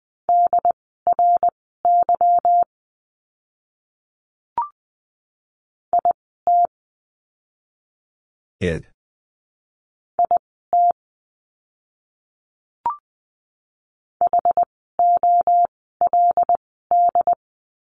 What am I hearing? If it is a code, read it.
Morse code, heard as a keyed tone, about 20 wpm, spaced as for 15 wpm, 700 Hz. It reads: DRY IT IT HOLD
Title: 146 Lesson 24f - Letter y - Words 20wpm